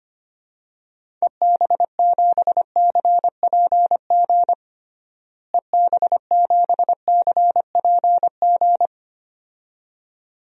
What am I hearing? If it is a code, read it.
Morse code, heard as a keyed tone, about 25 wpm, 700 Hz. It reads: EB7CPG EB7CPG
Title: DXCC-100 Call Signs - 6 Characters - Repeated 2x - 25wpm